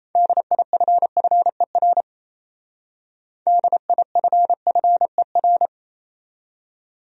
Morse code, heard as a keyed tone, about 34 wpm, 700 Hz. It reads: DIFFER DIFFER
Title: Speed-Racing Top 300 Words 28wpm